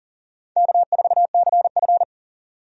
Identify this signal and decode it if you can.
Morse code, heard as a keyed tone, about 40 wpm, 700 Hz. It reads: K4CF
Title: USA Call Signs - 4 Characters - 40wpm